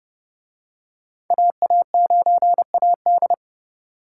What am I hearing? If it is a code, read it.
Morse code, heard as a keyed tone, about 29 wpm, 700 Hz. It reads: AA9AD